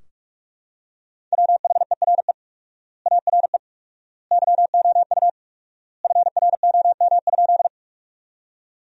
Morse code, heard as a keyed tone, about 46 wpm, 700 Hz. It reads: WHERE ARE YOU FROM?
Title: 4x Word-Spacing - Sentences from Top 200 Words - 45wpm